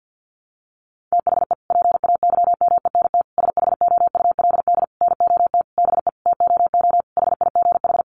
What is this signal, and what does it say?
Morse code, heard as a keyed tone, about 50 wpm, 700 Hz. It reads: THE PAYMENT SHOULD NOT BE TOO HIGH
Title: Sentences from Top 1200 Words 50wpm